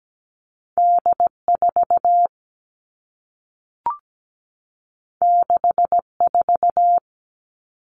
Morse code, heard as a keyed tone, about 17 wpm, 700 Hz. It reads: D 4 6 4
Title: Two Challenging Letter-Number Combinations 17wpm